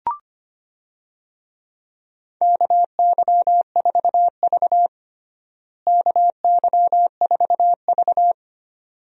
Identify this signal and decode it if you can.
Morse code, heard as a keyed tone, about 25 wpm, 700 Hz. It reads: KY4V KY4V
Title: USA Call Signs - 4 Characters - Repeated 2x - 25wpm